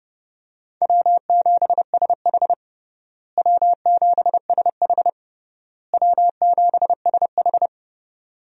Morse code, heard as a keyed tone, about 30 wpm, 700 Hz. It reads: W7SH W7SH W7SH
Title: USA Call Signs - 4 Characters - Repeated 3x - 30wpm